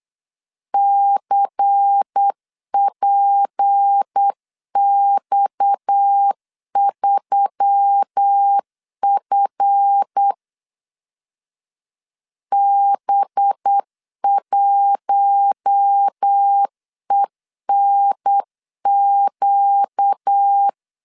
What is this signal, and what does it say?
Morse code, heard as a keyed tone, about 8 wpm, 800 Hz. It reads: CPX3F B1ENQ
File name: LEZIONI / 7 GF9Z / 7 GF 8wpm 3108